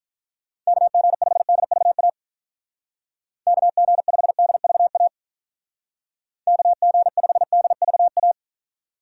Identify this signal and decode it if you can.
Morse code, heard as a keyed tone, about 46 wpm, 700 Hz. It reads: KG5DVA KG5DVA KG5DVA